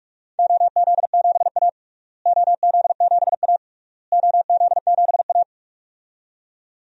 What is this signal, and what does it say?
Morse code, heard as a keyed tone, about 46 wpm, 700 Hz. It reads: OZ7A OZ7A OZ7A
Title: DXCC-100 Call Signs - 4 Characters - Repeated 3x - 45wpm